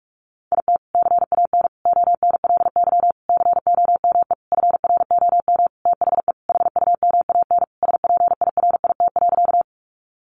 Morse code, heard as a keyed tone, about 44 wpm, 700 Hz. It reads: IT CAN ONLY COME FROM THE HUMAN SPIRIT.
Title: Sentences from Top 1400 Words 45wpm